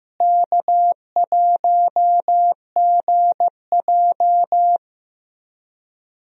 Morse code, heard as a keyed tone, about 15 wpm, 700 Hz. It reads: K1GJ
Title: USA Call Signs - 4 Characters - Repeated 2x - 15wpm